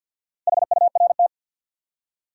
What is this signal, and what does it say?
Morse code, heard as a keyed tone, about 51 wpm, 700 Hz. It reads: HURT
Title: Four Letter Words - 50wpm